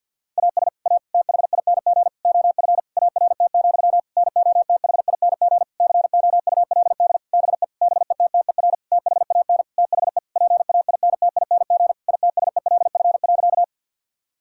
Morse code, heard as a keyed tone, about 52 wpm, 700 Hz. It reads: AS A THING OF ART, NOTHING COULD BE BETTER THAN THE PAINTING ITSELF.
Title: Sentences from Top 1000 Words - Review Part 6 - 50wpm